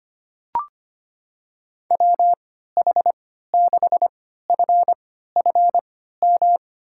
Morse code, heard as a keyed tone, about 25 wpm, spaced as for 14 wpm, 700 Hz. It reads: WH6FFM
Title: USA Call Signs - 3x Character Spacing - 25wpm